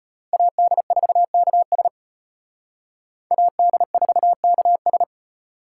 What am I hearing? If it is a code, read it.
Morse code, heard as a keyed tone, about 31 wpm, 700 Hz. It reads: AD4KS AD4KS
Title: Speed Racing - USA 2x2 Call Signs 25wpm